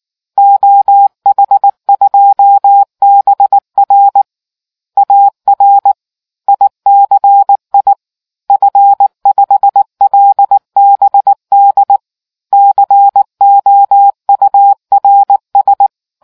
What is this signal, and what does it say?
Morse code, heard as a keyed tone, about 19 wpm, 800 Hz. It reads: OH2BR AR ICI F5LBD COURS